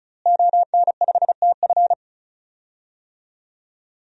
Morse code, heard as a keyed tone, about 36 wpm, 700 Hz. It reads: ON5TF
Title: DXCC-100 Call Signs - 5 Characters - Repeated 2x - 35wpm